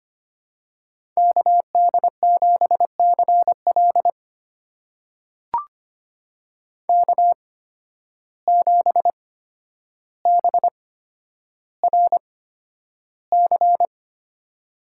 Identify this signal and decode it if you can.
Morse code, heard as a keyed tone, about 25 wpm, spaced as for 11 wpm, 700 Hz. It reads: KD7CL K7BRC
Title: USA Call Signs - 5 Characters - 8x Character Spacing - 25wpm